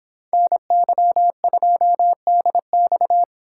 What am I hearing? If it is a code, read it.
Morse code, heard as a keyed tone, about 26 wpm, 700 Hz. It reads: NY2DX
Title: Speed Racing - USA 2x2 Call Signs 22wpm